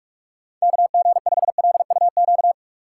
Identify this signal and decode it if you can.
Morse code, heard as a keyed tone, about 46 wpm, 700 Hz. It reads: KG5LUQ